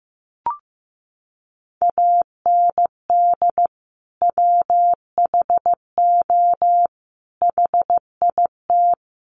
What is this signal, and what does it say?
Morse code, heard as a keyed tone, about 15 wpm, 700 Hz. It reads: AND WHO HIT
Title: Sentences from Top 1200 Words 15wpm